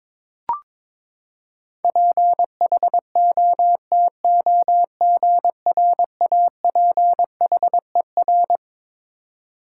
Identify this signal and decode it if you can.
Morse code, heard as a keyed tone, about 21 wpm, 700 Hz. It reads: PHOTOGRAPHER
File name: Twelve Letter Words - 22wpm